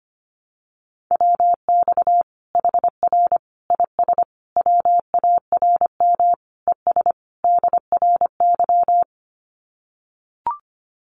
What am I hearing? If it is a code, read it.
Morse code, heard as a keyed tone, about 25 wpm, 700 Hz. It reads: WX HR IS WARM ES DRY
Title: Common QSO Phrases 25wpm